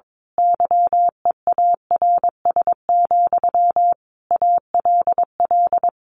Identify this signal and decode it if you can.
Morse code, heard as a keyed tone, about 22 wpm, 700 Hz. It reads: YEARS, ALL
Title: A Sound Of Thunder 22wpm